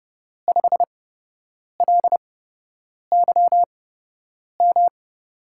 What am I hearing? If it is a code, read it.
Morse code, heard as a keyed tone, about 31 wpm, spaced as for 8 wpm, 700 Hz. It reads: 5LYM
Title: USA Call Signs - 6 Characters - 8x Character Spacing - 30wpm